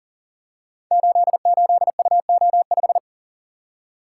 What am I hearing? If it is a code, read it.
Morse code, heard as a keyed tone, about 43 wpm, 690 Hz. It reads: 88UO5
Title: Code Groups - 5 Letter:Numbers - 40wpm